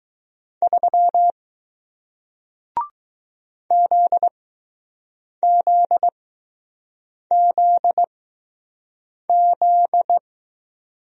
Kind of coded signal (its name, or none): Morse code